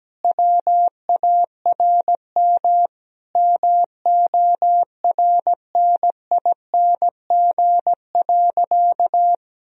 Morse code, heard as a keyed tone, about 17 wpm, 700 Hz. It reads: WARM MORNING.
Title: Sentences from Top 1000 Words - Review Part 2 - 17wpm